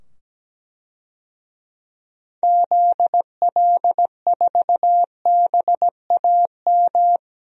Morse code, heard as a keyed tone, about 17 wpm, 700 Hz. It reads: ZL4BAM
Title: DXCC-100 Call Signs - 6 Characters - 17wpm